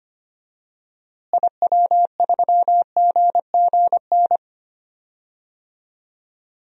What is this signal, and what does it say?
Morse code, heard as a keyed tone, about 25 wpm, 700 Hz. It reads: IW3GGN
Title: DXCC-100 Call Signs - 6 Characters - Repeated 2x - 25wpm